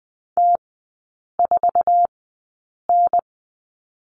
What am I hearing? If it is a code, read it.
Morse code, heard as a keyed tone, about 20 wpm, spaced as for 8 wpm, 700 Hz. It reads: T4N
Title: 047 Lesson 9e - Number 4 - Three Characters 20wpm